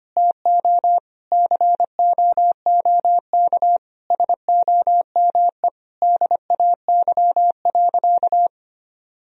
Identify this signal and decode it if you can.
Morse code, heard as a keyed tone, about 25 wpm, 700 Hz. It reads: TO COOK SOME DAY.
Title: Sentences from Top 1000 Words - Review Part 4 - 25wpm